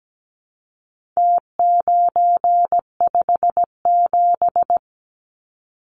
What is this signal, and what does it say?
Morse code, heard as a keyed tone, about 17 wpm, 700 Hz. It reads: T957
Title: Numbers - Contest Style - 4-Digit 17wpm